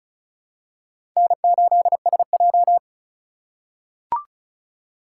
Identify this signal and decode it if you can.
Morse code, heard as a keyed tone, about 36 wpm, 700 Hz. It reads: N8SJ
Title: USA Call Signs - 4 Characters - 2x Character Spacing - 35wpm